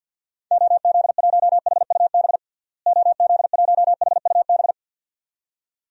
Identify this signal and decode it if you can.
Morse code, heard as a keyed tone, about 50 wpm, 700 Hz. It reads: OZ1HUB OZ1HUB
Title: DXCC-100 Call Signs - 6 Characters - Repeated 2x - 50wpm